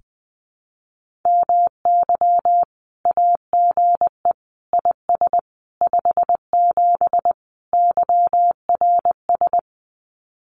Morse code, heard as a keyed tone, about 20 wpm, 700 Hz. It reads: MY AGE IS 57 YRS
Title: Common QSO Phrases Encore 20wpm